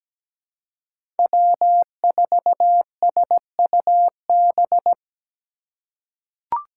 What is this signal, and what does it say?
Morse code, heard as a keyed tone, about 17 wpm, 700 Hz. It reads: W4SUB
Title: USA Call Signs - 5 Characters - 2x Character Spacing - 17wpm